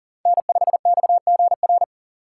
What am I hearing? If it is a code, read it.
Morse code, heard as a keyed tone, about 40 wpm, 700 Hz. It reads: N5XGR
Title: USA Call Signs - 5 Characters - 6x Character Spacing - 40wpm